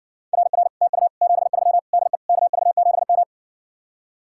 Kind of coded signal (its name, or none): Morse code